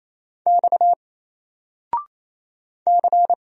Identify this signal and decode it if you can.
Morse code, heard as a keyed tone, about 27 wpm, 700 Hz. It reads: X C
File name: Single-Letter-Number-28wpm